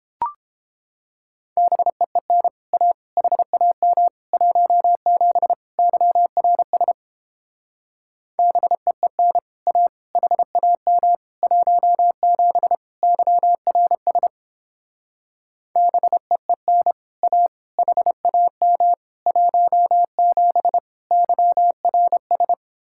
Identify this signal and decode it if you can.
Morse code, heard as a keyed tone, about 31 wpm, 700 Hz. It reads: BEEN A HAM 17 YRS BEEN A HAM 17 YRS BEEN A HAM 17 YRS